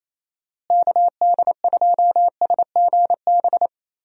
Morse code, heard as a keyed tone, about 28 wpm, 700 Hz. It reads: KD2SGB
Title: USA Call Signs - 6 Characters - Repeated 2x - 28wpm